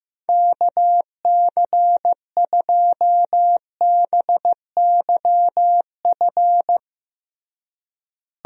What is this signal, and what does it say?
Morse code, heard as a keyed tone, about 16 wpm, 700 Hz. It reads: KC2BYF
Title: USA Call Signs - 6 Characters - Repeated 3x - 15wpm